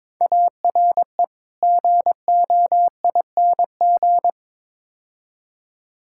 Morse code, heard as a keyed tone, about 22 wpm, 700 Hz. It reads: ARE GOING
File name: practice / 22wpm / Build A Phrase - 5-Word Phrases - 22wpm